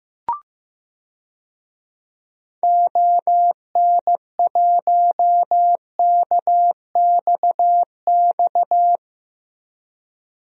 Morse code, heard as a keyed tone, about 15 wpm, 700 Hz. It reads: ON1KXX
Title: DXCC-100 Call Signs - 6 Characters - 15wpm